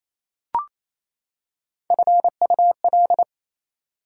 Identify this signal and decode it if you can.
Morse code, heard as a keyed tone, about 28 wpm, 700 Hz. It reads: FUL
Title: Prefixes and Suffixes 28wpm